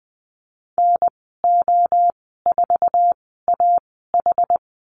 Morse code, heard as a keyed tone, about 20 wpm, spaced as for 14 wpm, 700 Hz. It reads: NO4AH
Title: 082 Lesson 14g - Number 2 - Call Signs 20wpm